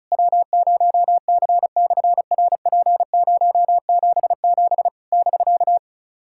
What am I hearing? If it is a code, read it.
Morse code, heard as a keyed tone, about 36 wpm, 700 Hz. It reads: W0C/RP077 <BK>
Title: SOTA QSO Practice 35wpm